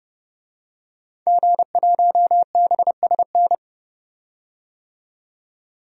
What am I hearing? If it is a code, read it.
Morse code, heard as a keyed tone, about 30 wpm, 700 Hz. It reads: G1BSN
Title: DXCC-100 Call Signs - 5 Characters - 30wpm